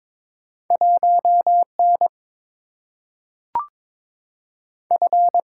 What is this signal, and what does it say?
Morse code, heard as a keyed tone, about 22 wpm, 700 Hz. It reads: 1 N F
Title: Single-Letter Single-Number 22wpm